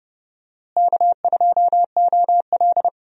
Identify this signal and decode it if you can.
Morse code, heard as a keyed tone, about 30 wpm, 700 Hz. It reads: K2OL
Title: USA Call Signs - 4 Characters - Repeated 3x - 30wpm